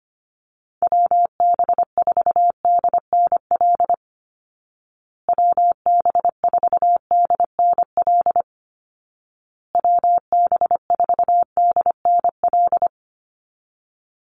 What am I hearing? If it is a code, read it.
Morse code, heard as a keyed tone, about 25 wpm, 700 Hz. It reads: WB4DNL WB4DNL WB4DNL
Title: Call Signs Repeated 2x - 25wpm